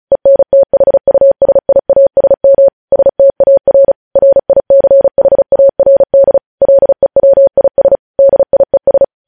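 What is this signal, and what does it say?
Morse code, heard as a keyed tone, about 36 wpm, 550 Hz. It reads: ENTHUSIASM STAR RICHARD LEWIS DIES
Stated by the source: News headlines sent in Morse code at 35 WPM.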